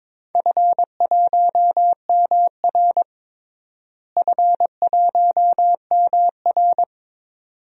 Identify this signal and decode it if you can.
Morse code, heard as a keyed tone, about 21 wpm, 700 Hz. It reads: F1MR F1MR
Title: DXCC-100 Call Signs - 4 Characters - Repeated 2x - 22wpm